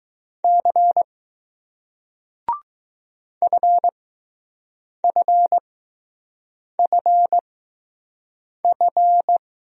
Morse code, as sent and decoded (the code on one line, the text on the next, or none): -.-. / ..-. ..-. ..-. ..-.
C FFFF